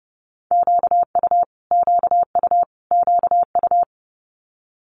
Morse code, heard as a keyed tone, about 30 wpm, 690 Hz. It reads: QU QU QU
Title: Multi-letter Phonemes Repeated 3x 30wpm